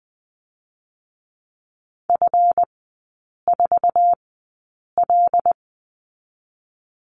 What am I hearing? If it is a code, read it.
Morse code, heard as a keyed tone, about 20 wpm, spaced as for 8 wpm, 700 Hz. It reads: F4L
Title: 218 Lesson 35e - Number 0 - Three Characters 20wpm